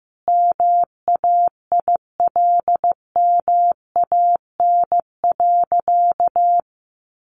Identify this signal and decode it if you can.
Morse code, heard as a keyed tone, about 15 wpm, 700 Hz. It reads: MAILMAN.
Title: Additional Easy Phrases 15wpm